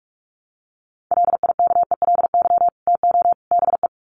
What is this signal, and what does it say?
Morse code, heard as a keyed tone, about 44 wpm, 700 Hz. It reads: LIKELY TO BE
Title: Sets of 3 Words 45wpm